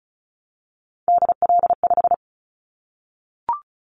Morse code, heard as a keyed tone, about 36 wpm, 700 Hz. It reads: DL5
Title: Callsign Prefixes for Top 50 DXCC - 35wpm